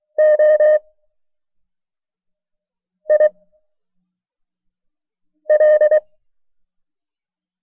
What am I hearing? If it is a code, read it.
Morse code, heard as a keyed tone, about 23 wpm, spaced as for 4 wpm, 610 Hz. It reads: OIL
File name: FreeFiles / CW-practice / KB6MT-tape1-side2